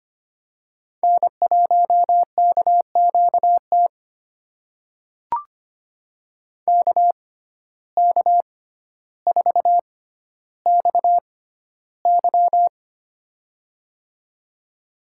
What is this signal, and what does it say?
Morse code, heard as a keyed tone, about 25 wpm, spaced as for 13 wpm, 700 Hz. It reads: N1KQT KK4XY
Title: USA Call Signs - 5 Characters - 6x Character Spacing - 25wpm